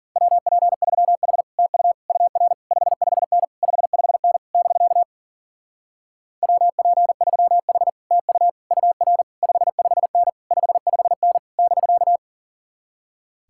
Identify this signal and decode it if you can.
Morse code, heard as a keyed tone, about 52 wpm, 700 Hz. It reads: WP3H TU UR 55N 55N <BK> WP3H TU UR 55N 55N <BK>